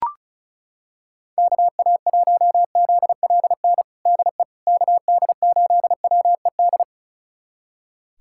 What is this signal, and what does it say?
Morse code, heard as a keyed tone, about 36 wpm, 700 Hz. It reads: KA1ZLN DE KD8WED
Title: USA Callsign DE Callsign - 6 Characters - 35wpm